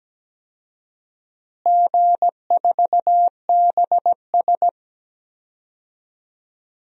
Morse code, heard as a keyed tone, about 17 wpm, 700 Hz. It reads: G4BS
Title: DXCC-100 Call Signs - 4 Characters - Repeated 3x - 17wpm